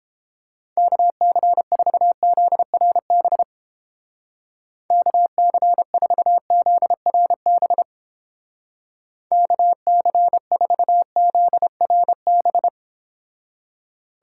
Morse code, heard as a keyed tone, about 30 wpm, 700 Hz. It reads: KC4ZRB KC4ZRB KC4ZRB